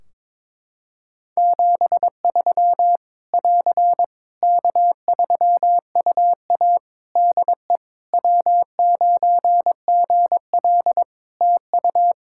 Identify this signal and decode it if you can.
Morse code, heard as a keyed tone, about 21 wpm, 700 Hz. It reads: 73 <AR> K3UA DE W9GL TU
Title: Practice QSO - Exchange 4 - Ending - 22wpm